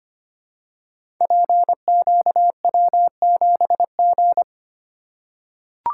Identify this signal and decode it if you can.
Morse code, heard as a keyed tone, about 25 wpm, 690 Hz. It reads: PQW7G
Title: Code Groups - 5 Letter:Numbers - 25wpm